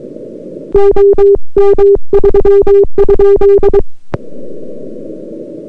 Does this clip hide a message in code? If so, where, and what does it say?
Yes: Morse code from 0.7 to 3.8 s, heard as a keyed tone, about 22 wpm, 400 Hz. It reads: OM3?